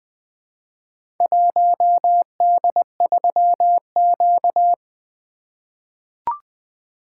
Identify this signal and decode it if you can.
Morse code, heard as a keyed tone, about 20 wpm, 700 Hz. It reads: 1D3Q
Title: Four Letter-Number Combinations - 20wpm